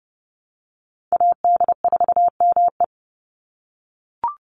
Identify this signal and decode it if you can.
Morse code, heard as a keyed tone, about 30 wpm, 700 Hz. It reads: AD4ME
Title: USA Call Signs 30wpm